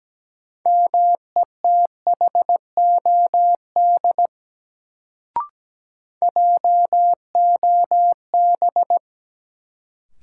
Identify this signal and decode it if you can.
Morse code, heard as a keyed tone, about 17 wpm, 700 Hz. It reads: METHOD JOB
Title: Sentences from Top 600 Words - New Word Drill - 17wpm